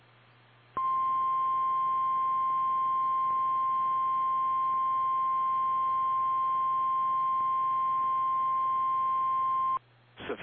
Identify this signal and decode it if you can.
Emergency Alert System (SAME) tones (the 1050 Hz attention tone). It is heard from 0.7 to 9.8 s.
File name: eas / KKPRKXPN / KKPRKXPN